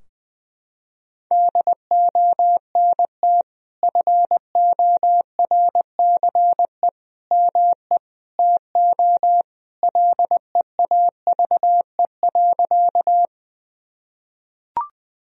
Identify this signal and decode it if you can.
Morse code, heard as a keyed tone, about 19 wpm, 700 Hz. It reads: DONT FORCE ME TO LEAVE.
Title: Sentences from Top 1000 Words - Review Part 3 - 20wpm